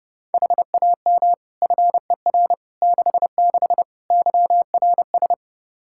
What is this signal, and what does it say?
Morse code, heard as a keyed tone, about 30 wpm, 700 Hz. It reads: HAM FER 66 YRS
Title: Practice QSO - Exchange 3 - More Info - 30wpm